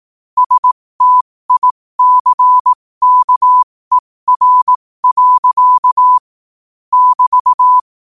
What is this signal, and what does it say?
Morse code, heard as a keyed tone, about 18 wpm, spaced as for 15 wpm, 1000 Hz. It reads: STICKER. <BT>
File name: qotdfiles / 15wpm